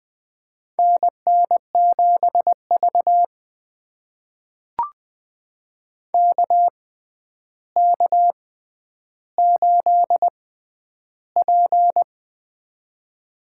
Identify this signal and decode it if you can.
Morse code, heard as a keyed tone, about 21 wpm, spaced as for 10 wpm, 700 Hz. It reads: NN7V KK8P
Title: USA Call Signs - 6x Character Spacing - 20wpm